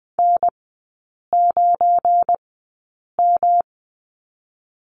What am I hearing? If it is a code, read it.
Morse code, heard as a keyed tone, about 20 wpm, spaced as for 8 wpm, 690 Hz. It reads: N9M
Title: 172 Lesson 28e - Number 9 - Three Characters 20wpm